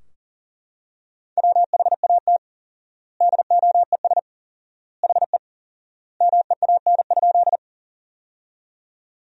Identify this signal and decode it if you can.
Morse code, heard as a keyed tone, about 40 wpm, 700 Hz. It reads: WHAT DOES HE MEAN?
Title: 4x Word-Spacing - Sentences from Top 200 Words - 40wpm